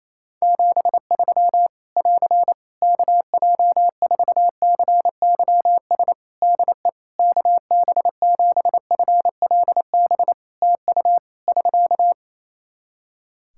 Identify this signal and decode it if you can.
Morse code, heard as a keyed tone, about 28 wpm, 700 Hz. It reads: 73 <AR> KJ4CYS DE KB7FLB TU <SK>
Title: Practice QSO - Exchange 4 - Ending - 28wpm